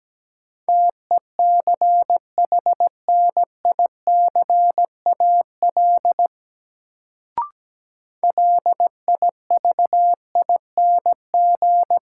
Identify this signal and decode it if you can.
Morse code, heard as a keyed tone, about 17 wpm, 700 Hz. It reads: TECHNICAL LIVING
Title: Sentences from Top 1100 Words - New Word Drill Alt - 17wpm